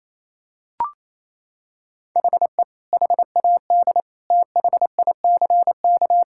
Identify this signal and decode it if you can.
Morse code, heard as a keyed tone, about 28 wpm, 700 Hz. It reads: HE HAD THICK
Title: Sentences from Top 1600 Words 28wpm